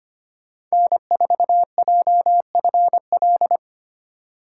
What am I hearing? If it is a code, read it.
Morse code, heard as a keyed tone, about 25 wpm, 700 Hz. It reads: N4JFL
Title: USA Call Signs - 5 Characters - 6x Character Spacing - 25wpm